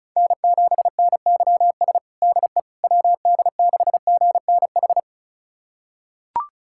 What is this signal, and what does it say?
Morse code, heard as a keyed tone, about 36 wpm, 700 Hz. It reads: N7NYS DE WD6GNH
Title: USA Callsign DE Callsign 35wpm